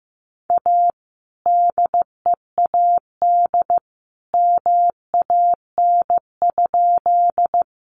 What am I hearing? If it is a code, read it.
Morse code, heard as a keyed tone, about 15 wpm, 700 Hz. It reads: A DEAD MAN?
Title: Sentences from Top 1500 Words 15wpm